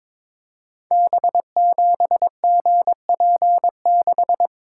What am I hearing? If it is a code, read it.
Morse code, heard as a keyed tone, about 22 wpm, 700 Hz. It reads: B7GP6
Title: Code Groups - 5 Letter:Numbers - 22wpm